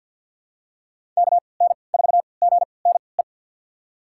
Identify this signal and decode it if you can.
Morse code, heard as a keyed tone, about 47 wpm, spaced as for 28 wpm, 700 Hz. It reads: KN4GNE
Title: USA Call Signs - 6 Characters - 3x Character Spacing - 50wpm